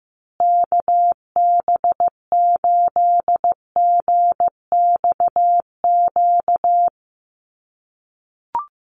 Morse code, heard as a keyed tone, about 14 wpm, 700 Hz. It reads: KB8GXQ
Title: USA Call Signs Repeated 2x 15wpm